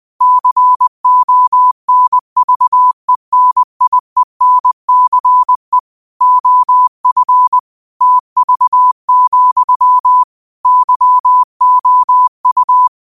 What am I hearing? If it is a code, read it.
Morse code, heard as a keyed tone, about 20 wpm, 1000 Hz. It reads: CONVENIENCE OF TV, YOU
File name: qotdfiles / 20wpm